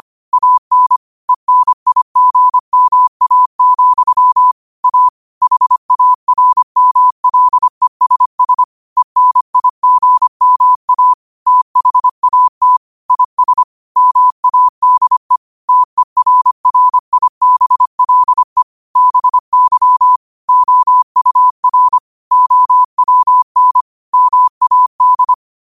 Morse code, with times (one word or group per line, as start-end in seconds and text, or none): 0.3-1.0 AN
1.3-4.5 ENIGMA,
4.8-5.1 A
5.4-8.6 HARMLESS
9.0-11.1 ENIGMA
11.5-12.8 THAT
13.1-13.6 IS
14.0-15.4 MADE
15.7-18.6 TERRIBLE
19.0-20.2 BY
20.5-22.0 OUR
22.3-23.8 OWN
24.1-25.3 MAD